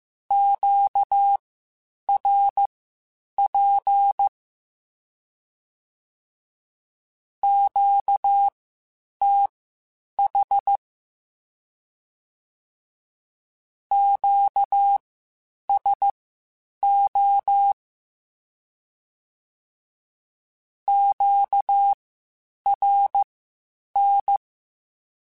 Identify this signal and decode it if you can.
Morse code, heard as a keyed tone, about 15 wpm, spaced as for 8 wpm, 800 Hz. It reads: QRP QTH QSO QRN
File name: (spaced) / Q-Calls Spaced-mp3s / Q Calls sp Once @13